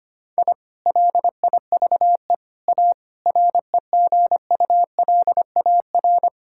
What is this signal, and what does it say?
Morse code, heard as a keyed tone, about 25 wpm, 700 Hz. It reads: I LIVE A REGULAR
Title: Sentences from Top 1000 Words - Review Part 6 - 25wpm